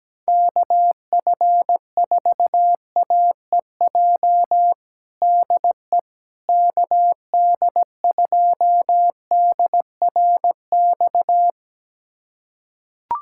Morse code, heard as a keyed tone, about 17 wpm, 700 Hz. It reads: KF4AEJ DE KD2DRX
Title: USA Callsign DE Callsign - 6 Characters - 17wpm